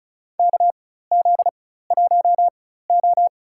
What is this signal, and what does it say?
Morse code, heard as a keyed tone, about 36 wpm, spaced as for 16 wpm, 700 Hz. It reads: KZ1O
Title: USA Call Signs - 4 Characters - 4x Character Spacing - 35wpm